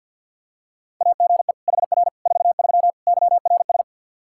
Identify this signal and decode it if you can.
Morse code, heard as a keyed tone, about 53 wpm, 700 Hz. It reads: AGE HR 43 YRS
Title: Practice QSO - Exchange 3 - More Info - 50wpm